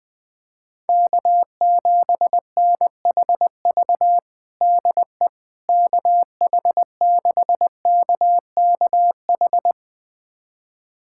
Morse code, heard as a keyed tone, about 20 wpm, 700 Hz. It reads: K7NHV DE KH6KKH
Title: USA Callsign DE Callsign 20wpm